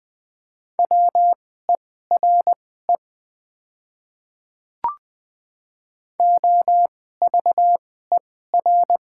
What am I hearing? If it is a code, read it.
Morse code, heard as a keyed tone, about 20 wpm, spaced as for 14 wpm, 700 Hz. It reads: WERE OVER
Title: 2x Character Spacing - Top 100 Words - 20wpm